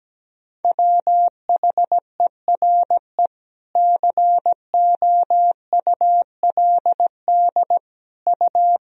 Morse code, heard as a keyed tone, about 17 wpm, 700 Hz. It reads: WHERE COULD U
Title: Sentences from Top 300 Words and Abbreviations 17wpm